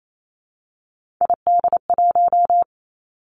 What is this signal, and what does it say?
Morse code, heard as a keyed tone, about 28 wpm, 700 Hz. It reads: ID1
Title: Three Challenging Letter-Number Combinations 28wpm